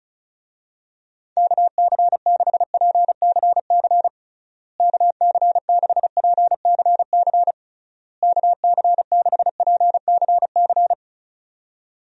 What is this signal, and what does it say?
Morse code, heard as a keyed tone, about 36 wpm, 700 Hz. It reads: KC6PCC KC6PCC KC6PCC
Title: USA Call Signs - 6 Characters - Repeated 3x - 35wpm